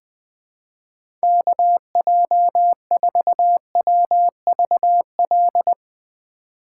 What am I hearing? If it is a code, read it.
Morse code, heard as a keyed tone, about 21 wpm, 700 Hz. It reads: KJ4WVL
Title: USA Call Signs - 6 Characters - 2x Character Spacing - 20wpm